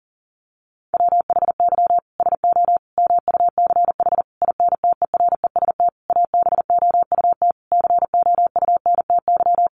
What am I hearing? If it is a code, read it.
Morse code, heard as a keyed tone, about 40 wpm, 700 Hz. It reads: WHY SO MUCH INTEREST ABOUT COUNTY
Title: Sentences from Top 1200 Words 40wpm